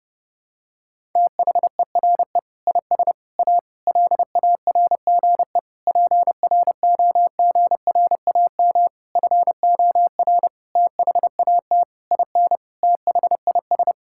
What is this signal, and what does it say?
Morse code, heard as a keyed tone, about 30 wpm, 700 Hz. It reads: THERE IS A LARGE PROGRAM FOR THAT IN THIS
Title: Sentences from Top 1000 Words - Review Part 5 - 30wpm